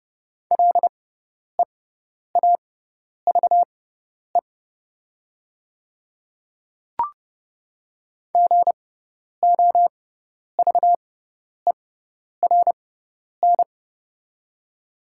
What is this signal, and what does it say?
Morse code, heard as a keyed tone, about 30 wpm, spaced as for 10 wpm, 700 Hz. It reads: LEAVE GOVERN
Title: 6x Character Spacing - Top 500 Words - 30wpm